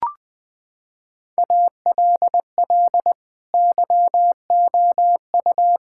Morse code, heard as a keyed tone, about 21 wpm, 700 Hz. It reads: ALL YOU
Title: Sentences with 5-Word Phrases - 20wpm